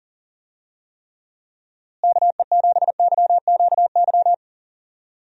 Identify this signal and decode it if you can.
Morse code, heard as a keyed tone, about 38 wpm, 700 Hz. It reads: KE7YQY
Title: USA Call Signs - 6 Characters - Repeated 3x - 40wpm